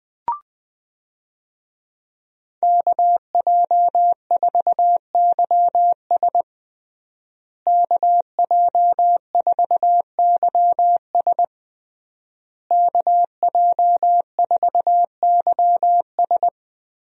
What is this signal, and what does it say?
Morse code, heard as a keyed tone, about 20 wpm, 700 Hz. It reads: KJ4YS KJ4YS KJ4YS
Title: USA Call Signs - 5 Characters - Repeated 3x - 20wpm